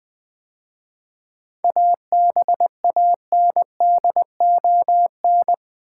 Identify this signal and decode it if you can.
Morse code, heard as a keyed tone, about 20 wpm, 700 Hz. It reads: ABANDON
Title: 7 Letter Words - First 3 Letters Given - 20wpm